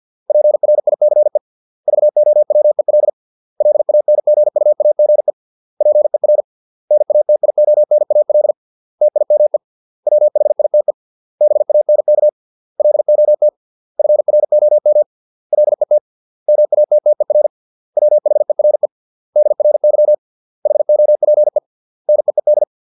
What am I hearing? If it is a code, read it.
Morse code, heard as a keyed tone, about 50 wpm, 600 Hz. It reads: PRICE VOWEL LANGUAGE PER NATIONAL TIME WHITE BANK LOT FROM LET MATTER WHERE DRY HOPE NEED